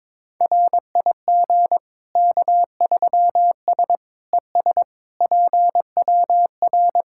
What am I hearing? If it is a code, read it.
Morse code, heard as a keyed tone, about 21 wpm, 700 Hz. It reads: RIG K3S ES PWR
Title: Practice QSO - Exchange 2 - Info - 22wpm